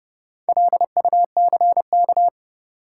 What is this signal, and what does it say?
Morse code, heard as a keyed tone, about 30 wpm, 700 Hz. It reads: LUCK
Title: Sentences from Top 1500 Words - New Word Drill Alt - 30wpm